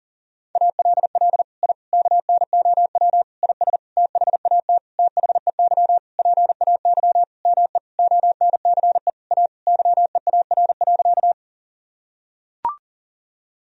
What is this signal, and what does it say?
Morse code, heard as a keyed tone, about 40 wpm, 700 Hz. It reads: ALL I KNOW IS THAT THEY PAY ME ONCE A YEAR.
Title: Sentences from Top 1000 Words - Review Part 1 - 40wpm